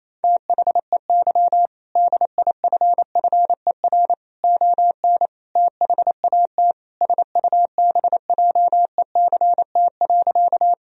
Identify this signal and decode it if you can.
Morse code, heard as a keyed tone, about 28 wpm, 700 Hz. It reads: THEY DIFFER ON THAT SUBJECT.
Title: Sentences from Top 1000 Words - Review Part 1 - 28wpm